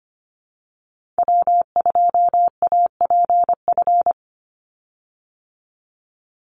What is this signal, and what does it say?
Morse code, heard as a keyed tone, about 25 wpm, 700 Hz. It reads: W2APF
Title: Call Signs Repeated 2x - 25wpm